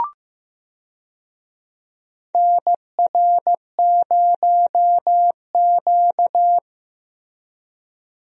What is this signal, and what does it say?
Morse code, heard as a keyed tone, about 15 wpm, 700 Hz. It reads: NR0Q
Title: USA Call Signs - 4 Characters - Repeated 2x - 15wpm